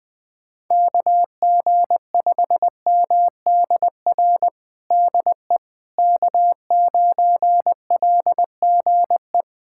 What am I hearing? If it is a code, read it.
Morse code, heard as a keyed tone, about 19 wpm, 700 Hz. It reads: KG5MDR DE K9LGE
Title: USA Callsign DE Callsign 20wpm